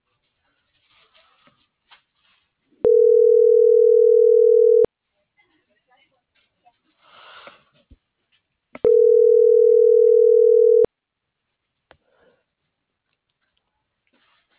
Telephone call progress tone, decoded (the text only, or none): ringback tone